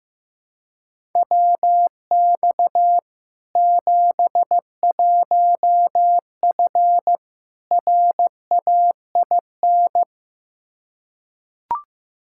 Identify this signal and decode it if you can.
Morse code, heard as a keyed tone, about 16 wpm, 700 Hz. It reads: WX 71F RAIN
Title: K1USN SST QSO Practice 15wpm